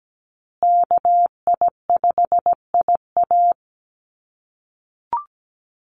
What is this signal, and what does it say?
Morse code, heard as a keyed tone, about 17 wpm, 700 Hz. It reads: KI5IA
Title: USA Call Signs Repeated 3x 17wpm